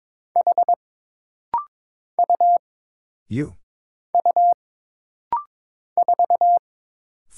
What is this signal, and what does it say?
Morse code, heard as a keyed tone, about 22 wpm, 700 Hz. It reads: H U U 4
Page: This rapid-fire timing is geared to push your ICR (Instant Character Recognition) skills.